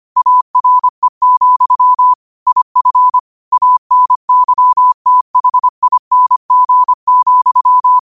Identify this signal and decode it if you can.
Morse code, heard as a keyed tone, about 25 wpm, 1000 Hz. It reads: ARE, IF ANYTHING,
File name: qotdfiles / 25wpm